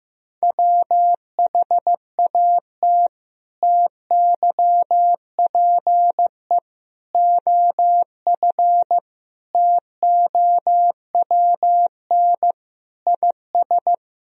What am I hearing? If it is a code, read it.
Morse code, heard as a keyed tone, about 15 wpm, 700 Hz. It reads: WHAT TYPE OF TOWN IS
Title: Sentences from Top 1000 Words - Review Part 3 - 15wpm